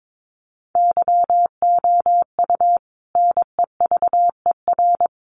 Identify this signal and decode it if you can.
Morse code, heard as a keyed tone, about 22 wpm, 700 Hz. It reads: YOU NEVER
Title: Sentences from Top 1500 Words 22wpm